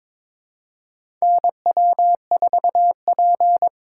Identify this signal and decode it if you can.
Morse code, heard as a keyed tone, about 22 wpm, 700 Hz. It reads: NW4P
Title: USA Call Signs - 4 Characters - 4x Character Spacing - 22wpm